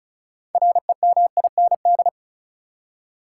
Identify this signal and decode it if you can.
Morse code, heard as a keyed tone, about 36 wpm, 700 Hz. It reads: REMIND
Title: Six Letter Words - 35wpm